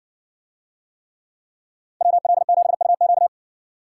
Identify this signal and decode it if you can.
Morse code, heard as a keyed tone, about 60 wpm, 700 Hz. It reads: WL7UQ